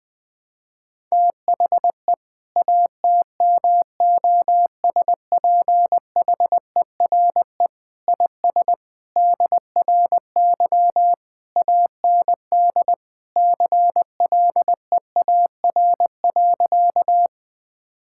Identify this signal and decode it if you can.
Morse code, heard as a keyed tone, about 20 wpm, 700 Hz. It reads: THE ATMOSPHERE IS DRY AND CLEAR.
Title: Sentences from Top 1600 Words 20wpm